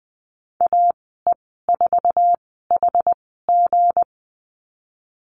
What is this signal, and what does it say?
Morse code, heard as a keyed tone, about 20 wpm, spaced as for 14 wpm, 700 Hz. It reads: AE4HG
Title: 167 Lesson 27g - Number 7 - Call Signs 20wpm